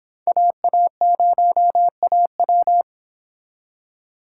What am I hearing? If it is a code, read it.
Morse code, heard as a keyed tone, about 28 wpm, 700 Hz. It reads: AA0AW